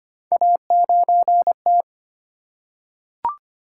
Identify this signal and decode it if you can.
Morse code, heard as a keyed tone, about 25 wpm, 700 Hz. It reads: A9T
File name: Three Letter-Number Combinations - 25wpm